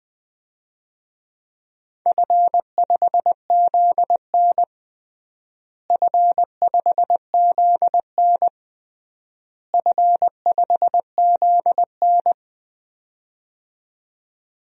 Morse code, heard as a keyed tone, about 20 wpm, 700 Hz. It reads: F5ZN F5ZN F5ZN
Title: DXCC-100 Call Signs - 4 Characters - Repeated 3x - 20wpm